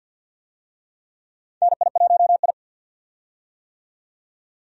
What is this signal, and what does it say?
Morse code, heard as a keyed tone, about 52 wpm, 700 Hz. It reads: NI1I
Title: USA Call Signs - 4 Characters - 50wpm